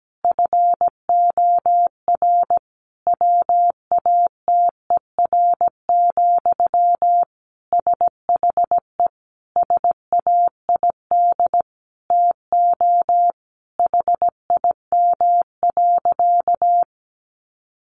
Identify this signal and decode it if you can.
Morse code, heard as a keyed tone, about 17 wpm, 700 Hz. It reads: FOR WATER, SHE SAID TO HIM.
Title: Sentences from Top 800 Words 17wpm